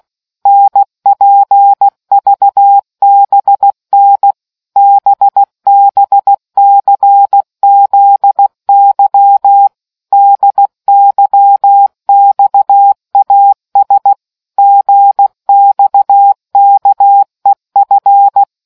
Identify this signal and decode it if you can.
Morse code, heard as a keyed tone, about 16 wpm, 800 Hz. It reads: NPVBN BBCZY DYXAS GXKEF